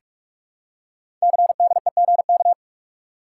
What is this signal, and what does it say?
Morse code, heard as a keyed tone, about 44 wpm, 700 Hz. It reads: CDEGK
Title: Code Groups - 5 Letters - 45wpm